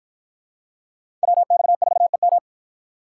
Morse code, heard as a keyed tone, about 52 wpm, 700 Hz. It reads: WX4EM